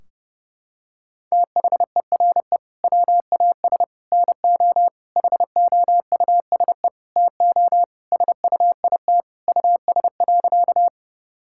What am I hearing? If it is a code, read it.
Morse code, heard as a keyed tone, about 30 wpm, 700 Hz. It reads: THERE WAS NO HOUSE TO SUIT US.
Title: Sentences from Top 1600 Words 30wpm